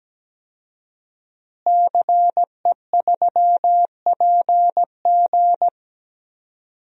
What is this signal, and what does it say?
Morse code, heard as a keyed tone, about 17 wpm, 700 Hz. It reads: CE3PG
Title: DXCC-100 Call Signs - 5 Characters - Repeated 2x - 17wpm